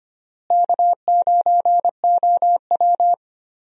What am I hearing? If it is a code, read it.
Morse code, heard as a keyed tone, about 25 wpm, 700 Hz. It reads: K9OW
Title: USA Call Signs - 4 Characters - Repeated 3x - 25wpm